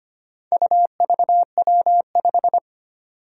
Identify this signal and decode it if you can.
Morse code, heard as a keyed tone, about 25 wpm, 700 Hz. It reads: UVW5
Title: Four Challenging Letter-Number Combinations 25wpm